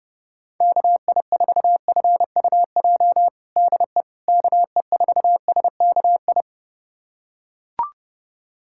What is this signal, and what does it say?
Morse code, heard as a keyed tone, about 30 wpm, 700 Hz. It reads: KI4FUJ DE KE4SKI
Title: USA Callsign DE Callsign - 6 Characters - 30wpm